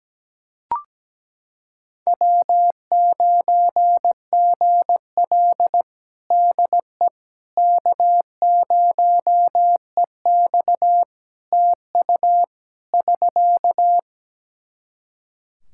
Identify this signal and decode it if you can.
Morse code, heard as a keyed tone, about 17 wpm, 700 Hz. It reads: W9GL DE K0EX TU <SK>
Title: Practice QSO - Exchange 4 - Ending - 17wpm